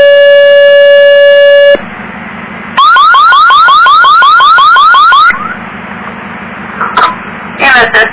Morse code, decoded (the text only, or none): none